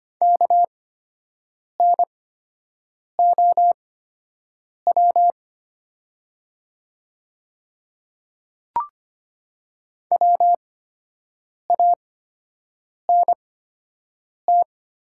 Morse code, heard as a keyed tone, about 25 wpm, spaced as for 7 wpm, 700 Hz. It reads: KNOW WANT
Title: 8x Character Spacing - Top 300 Words - 25wpm